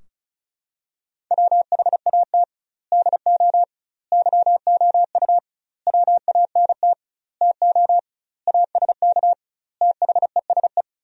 Morse code, heard as a keyed tone, about 36 wpm, 700 Hz. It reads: WHAT DO YOU WANT TO ASK THESE
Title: 2x Word-Spacing - Sentences from Top 200 Words - 35wpm